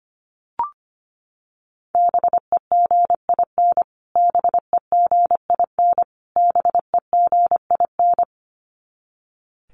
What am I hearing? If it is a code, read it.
Morse code, heard as a keyed tone, about 25 wpm, 700 Hz. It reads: BEGIN BEGIN BEGIN
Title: Top 500 Words Repeated 25wpm